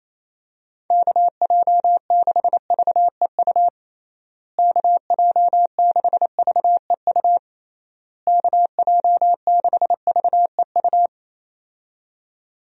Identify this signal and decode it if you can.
Morse code, heard as a keyed tone, about 28 wpm, 700 Hz. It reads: KJ6VEU KJ6VEU KJ6VEU
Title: USA Call Signs - 6 Characters - Repeated 3x - 28wpm